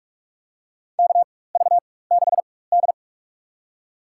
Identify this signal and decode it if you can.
Morse code, heard as a keyed tone, about 42 wpm, spaced as for 20 wpm, 700 Hz. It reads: KV6D
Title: USA Call Signs - 4x Character Spacing - 45wpm